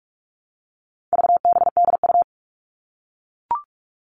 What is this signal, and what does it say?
Morse code, heard as a keyed tone, about 44 wpm, 700 Hz. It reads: VBDU
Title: All Cut Numbers - 4-Digit 45wpm